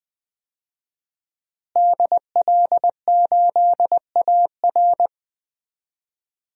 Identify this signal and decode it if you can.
Morse code, heard as a keyed tone, about 20 wpm, 700 Hz. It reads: DL8AR
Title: DXCC-100 Call Signs - 5 Characters - Repeated 3x - 20wpm